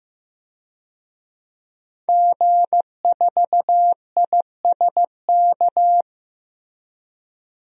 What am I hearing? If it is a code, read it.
Morse code, heard as a keyed tone, about 15 wpm, 700 Hz. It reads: G4ISK
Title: DXCC-100 Call Signs - 5 Characters - 15wpm